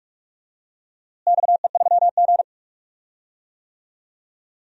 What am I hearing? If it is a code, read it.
Morse code, heard as a keyed tone, about 47 wpm, 700 Hz. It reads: XE3G
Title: DXCC-100 Call Signs - 4 Characters - Repeated 3x - 45wpm